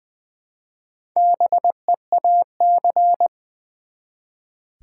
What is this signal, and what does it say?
Morse code, heard as a keyed tone, about 20 wpm, 700 Hz. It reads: BEAC
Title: Build A Word - 1 Letter At A Time - 5-Letter Words - 20wpm